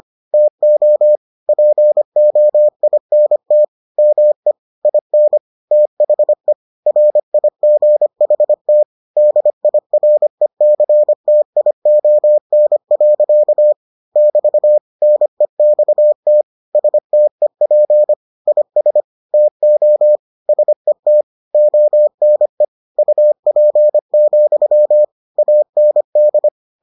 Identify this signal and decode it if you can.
Morse code, heard as a keyed tone, about 25 wpm, 600 Hz. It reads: TO POINT ME IN THE RIGHT DIRECTION. <BT> NEXT STEP IS TO SET ONE UP, AND